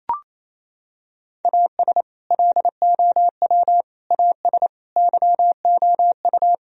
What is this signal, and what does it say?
Morse code, heard as a keyed tone, about 28 wpm, 700 Hz. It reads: AS LOW AS YOU
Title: Sentences from Top 1000 Words - Review Part 1 - 28wpm